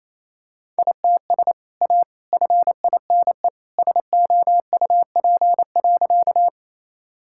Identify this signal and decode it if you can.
Morse code, heard as a keyed tone, about 28 wpm, 700 Hz. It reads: ITS A FINE SOUP.
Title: Sentences from Top 1000 Words - Review Part 6 - 28wpm